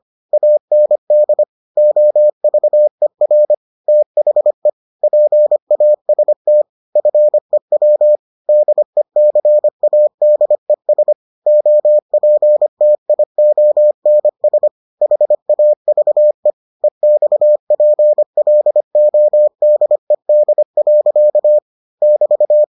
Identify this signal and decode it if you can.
Morse code, heard as a keyed tone, about 25 wpm, 600 Hz. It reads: AND OVER THE PAST FEW DECADES OPTIONS HAVE EXPLODED. <BT>